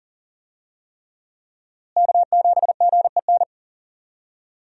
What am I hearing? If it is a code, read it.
Morse code, heard as a keyed tone, about 40 wpm, 700 Hz. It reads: K7GEN
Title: USA Call Signs - 5 Characters - Repeated 3x - 40wpm